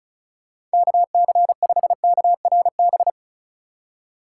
Morse code, heard as a keyed tone, about 34 wpm, 690 Hz. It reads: KC5KRB
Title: USA Call Signs - 4x Character Spacing - 35wpm